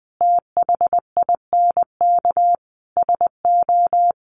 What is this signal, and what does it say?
Morse code, heard as a keyed tone, about 20 wpm, 700 Hz. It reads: THINK SO
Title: Sets of 4 Words 20wpm